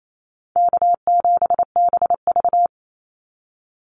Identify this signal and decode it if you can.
Morse code, heard as a keyed tone, about 28 wpm, 700 Hz. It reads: K7BV
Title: Call Signs - 28wpm